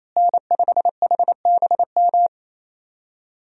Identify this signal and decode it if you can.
Morse code, heard as a keyed tone, about 28 wpm, 700 Hz. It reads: N5HBM
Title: USA Call Signs - 5 Characters - Repeated 3x - 28wpm